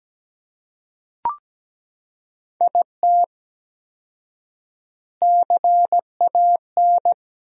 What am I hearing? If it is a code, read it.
Morse code, heard as a keyed tone, about 17 wpm, 700 Hz. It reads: IT CAN
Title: 4x Word-Spacing - Sentences from Top 400 Words - 17wpm